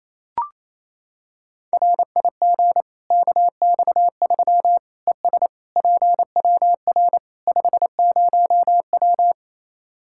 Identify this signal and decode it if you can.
Morse code, heard as a keyed tone, about 28 wpm, 700 Hz. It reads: RIG KX3 ES PWR 50W
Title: Practice QSO - Exchange 2 - Info - 28wpm